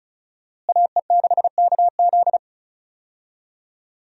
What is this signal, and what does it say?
Morse code, heard as a keyed tone, about 34 wpm, 700 Hz. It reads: AE6KZ
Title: USA Call Signs - 5 Characters - Repeated 2x - 35wpm